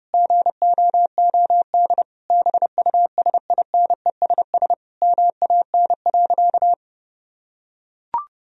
Morse code, heard as a keyed tone, about 30 wpm, 700 Hz. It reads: GOOD BUSINESS MAN.
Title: Sentences from Top 1000 Words - Review Part 4 - 30wpm